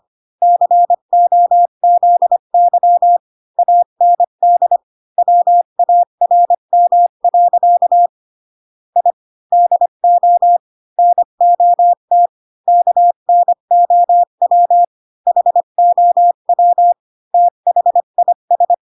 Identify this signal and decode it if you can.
Morse code, heard as a keyed tone, about 25 wpm, 700 Hz. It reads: COZY AND WARM. I DO NOT KNOW HOW THIS
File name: Pinocchio / Pinocchio ch1 20 25 ik1hge